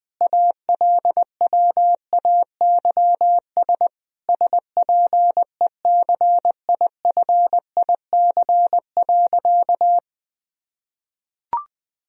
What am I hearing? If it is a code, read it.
Morse code, heard as a keyed tone, about 21 wpm, 700 Hz. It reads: ALWAYS SPECIFIC.
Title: Sentences from Top 1000 Words - Review Part 3 - 20wpm